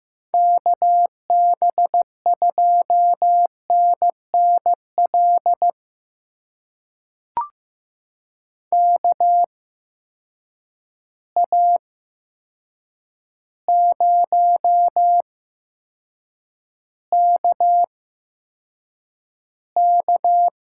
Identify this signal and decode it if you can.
Morse code, heard as a keyed tone, about 15 wpm, spaced as for 7 wpm, 700 Hz. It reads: KB2NNL KA0KK
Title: USA Call Signs - 6 Characters - 8x Character Spacing - 15wpm